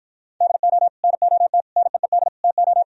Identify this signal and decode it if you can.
Morse code, heard as a keyed tone, about 53 wpm, 700 Hz. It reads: DO NOT NEED TO